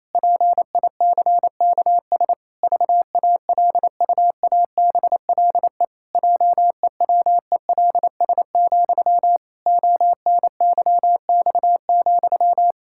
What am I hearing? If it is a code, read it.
Morse code, heard as a keyed tone, about 28 wpm, 700 Hz. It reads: PICKS VALUABLE JEWELS, ONYX,
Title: Pangram Challenge v2 - 28wpm